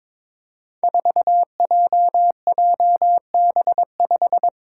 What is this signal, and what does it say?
Morse code, heard as a keyed tone, about 22 wpm, 700 Hz. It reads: 4JJB5
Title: Code Groups - 5 Letter:Numbers - 22wpm